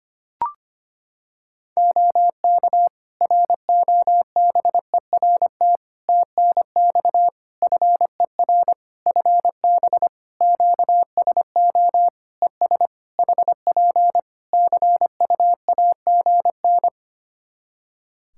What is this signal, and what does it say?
Morse code, heard as a keyed tone, about 25 wpm, 700 Hz. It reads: OK ROBERT TNX FER FB QSO ES HP CUAGN
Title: Practice QSO - Exchange 4 - Ending - 25wpm